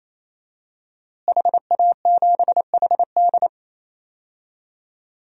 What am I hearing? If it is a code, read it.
Morse code, heard as a keyed tone, about 28 wpm, 700 Hz. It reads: HA7HD
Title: DXCC-100 Call Signs - 5 Characters - Repeated 3x - 28wpm